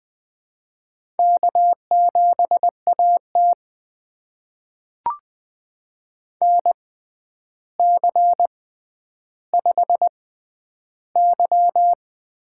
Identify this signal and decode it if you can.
Morse code, heard as a keyed tone, about 20 wpm, spaced as for 10 wpm, 700 Hz. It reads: K7AT NC5Y
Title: USA Call Signs - 4 Characters - 6x Character Spacing - 20wpm